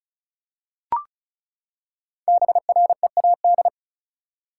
Morse code, heard as a keyed tone, about 36 wpm, 700 Hz. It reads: BREAD
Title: Five Letter Words - 35wpm